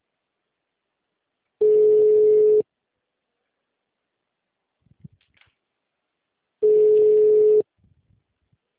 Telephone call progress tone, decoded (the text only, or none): ringback tone